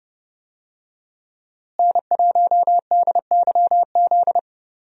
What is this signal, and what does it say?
Morse code, heard as a keyed tone, about 30 wpm, 700 Hz. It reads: N1DYZ
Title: USA Call Signs - 5 Characters - Repeated 3x - 30wpm